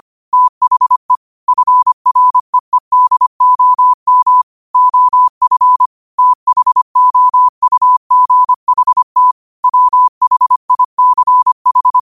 Morse code, heard as a keyed tone, about 25 wpm, 1000 Hz. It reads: THE FREEDOM OF THOUGHT WHICH
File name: qotdfiles / 25wpm